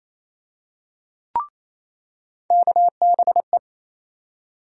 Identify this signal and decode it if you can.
Morse code, heard as a keyed tone, about 28 wpm, 700 Hz. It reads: KBE
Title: USA Call Sign Postfixes - 3 Letters 28wpm